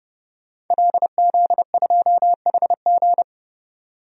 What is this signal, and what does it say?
Morse code, heard as a keyed tone, about 30 wpm, 700 Hz. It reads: LZ2HG
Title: DXCC-100 Call Signs - 5 Characters - 30wpm